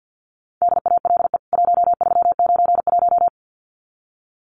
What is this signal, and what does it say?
Morse code, heard as a keyed tone, about 50 wpm, 700 Hz. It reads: DALE 1391
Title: CWT Exchange 50wpm